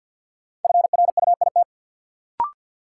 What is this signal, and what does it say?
Morse code, heard as a keyed tone, about 53 wpm, 700 Hz. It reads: FRUIT
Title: Sentences from Top 1300 Words - New Word Drill Alt - 50wpm